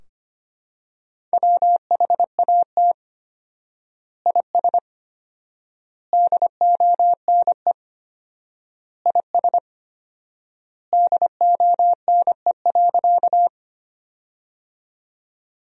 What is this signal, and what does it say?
Morse code, heard as a keyed tone, about 25 wpm, 700 Hz. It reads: WHAT IS DONE IS DONE.
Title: 4x Word-Spacing - Sentences from Top 500 Words - 25wpm